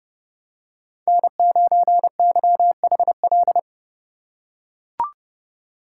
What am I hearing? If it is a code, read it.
Morse code, heard as a keyed tone, about 30 wpm, 700 Hz. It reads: N9YHL